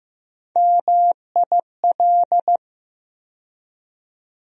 Morse code, heard as a keyed tone, about 15 wpm, 700 Hz. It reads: MIL
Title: USA Call Sign Postfixes - 3 Letters 15wpm